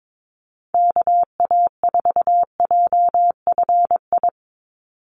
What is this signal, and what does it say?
Morse code, heard as a keyed tone, about 22 wpm, 700 Hz. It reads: KA4JFI
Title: USA Call Signs 22wpm